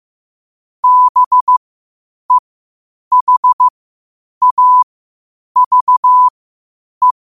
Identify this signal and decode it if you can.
Morse code, heard as a keyed tone, about 15 wpm, spaced as for 8 wpm, 1000 Hz. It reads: BEHAVE
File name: qotdfiles / 8wpm